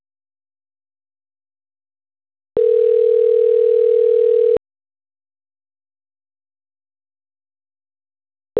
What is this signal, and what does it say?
Ringback tone (North America), 440 + 480 Hz.